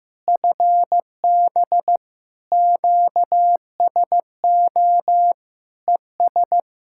Morse code, heard as a keyed tone, about 15 wpm, 700 Hz. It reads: FB QSO ES
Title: Practice QSO - Exchange 4 - Ending - 15wpm